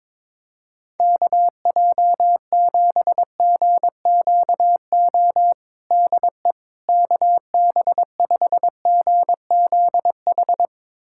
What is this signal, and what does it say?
Morse code, heard as a keyed tone, about 22 wpm, 700 Hz. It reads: KJ7GQO DE KB5GZH
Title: USA Callsign DE Callsign - 6 Characters - 22wpm